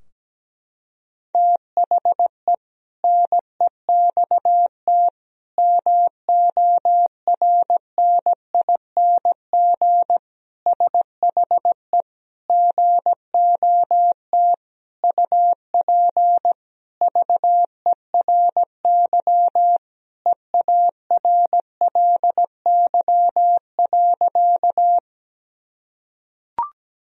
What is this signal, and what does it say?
Morse code, heard as a keyed tone, about 17 wpm, 700 Hz. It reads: THE NEXT MORNING SHE GOT UP VERY EARLY.
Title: Sentences from Top 1000 Words - Review Part 2 - 17wpm